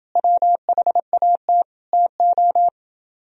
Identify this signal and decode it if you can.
Morse code, heard as a keyed tone, about 29 wpm, 700 Hz. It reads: WHAT TO